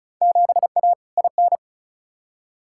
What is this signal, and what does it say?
Morse code, heard as a keyed tone, about 36 wpm, 700 Hz. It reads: 7A IN
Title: Field Day Exchanges Encore - 35wpm